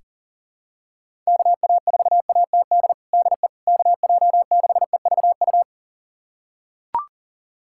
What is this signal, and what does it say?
Morse code, heard as a keyed tone, about 40 wpm, 700 Hz. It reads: KA4ATD DE KJ6EVU
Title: USA Callsign DE Callsign - 6 Characters - 40wpm